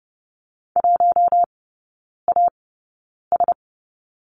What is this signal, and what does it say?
Morse code, heard as a keyed tone, about 30 wpm, spaced as for 9 wpm, 690 Hz. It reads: 1AS
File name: Lesson 17e - Letter u - Three Characters 30wpm